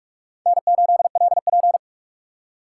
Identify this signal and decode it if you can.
Morse code, heard as a keyed tone, about 39 wpm, 700 Hz. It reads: N8LP